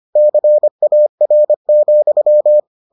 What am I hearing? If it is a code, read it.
Morse code, heard as a keyed tone, about 25 wpm, 600 Hz. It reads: CAR,